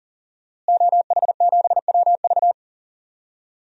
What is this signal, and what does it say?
Morse code, heard as a keyed tone, about 40 wpm, 700 Hz. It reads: OH7WV
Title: DXCC-100 Call Signs - 5 Characters - Repeated 3x - 40wpm